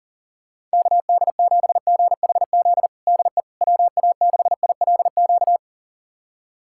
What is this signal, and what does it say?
Morse code, heard as a keyed tone, about 40 wpm, 700 Hz. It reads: KD7GHZ DE WA6ILQ
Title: USA Callsign DE Callsign - 6 Characters - 40wpm